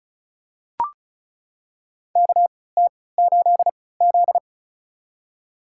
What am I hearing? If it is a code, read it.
Morse code, heard as a keyed tone, about 36 wpm, spaced as for 20 wpm, 700 Hz. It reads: KT8Z
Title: USA Call Signs - 4 Characters - 3x Character Spacing - 35wpm